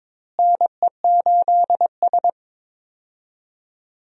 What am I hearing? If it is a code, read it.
Morse code, heard as a keyed tone, about 21 wpm, 700 Hz. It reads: NE8S
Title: USA Call Signs - 4 Characters - Repeated 3x - 22wpm